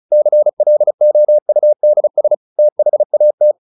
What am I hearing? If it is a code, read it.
Morse code, heard as a keyed tone, about 34 wpm, 600 Hz. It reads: CLOUDS THAT
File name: the old man and the sea / the old man and the sea 1 35wpm